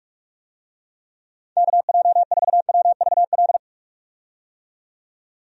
Morse code, heard as a keyed tone, about 44 wpm, 700 Hz. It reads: KJ4WVL
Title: USA Call Signs - 6 Characters - 45wpm